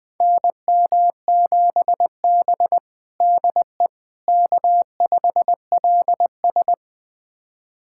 Morse code, heard as a keyed tone, about 20 wpm, 700 Hz. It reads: NM7B DE K5LS
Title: USA Callsign DE Callsign - 4 Characters - 20wpm